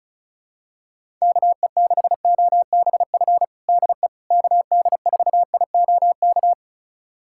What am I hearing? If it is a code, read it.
Morse code, heard as a keyed tone, about 36 wpm, 700 Hz. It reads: KE6OBF DE KD4IOK
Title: USA Callsign DE Callsign 35wpm